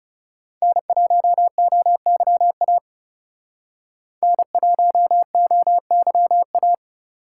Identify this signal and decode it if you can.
Morse code, heard as a keyed tone, about 36 wpm, 700 Hz. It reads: N1OYA N1OYA
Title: Speed Racing - USA 1x3 Call Signs 30wpm